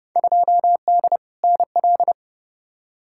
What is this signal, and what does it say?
Morse code, heard as a keyed tone, about 30 wpm, 700 Hz. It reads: 2D NL
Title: Field Day Exchanges Encore - 30wpm